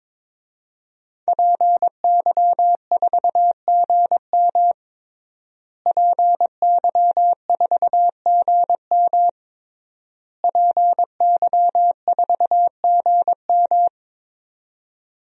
Morse code, heard as a keyed tone, about 22 wpm, 700 Hz. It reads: PY4GM PY4GM PY4GM
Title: DXCC-100 Call Signs - 5 Characters - Repeated 3x - 22wpm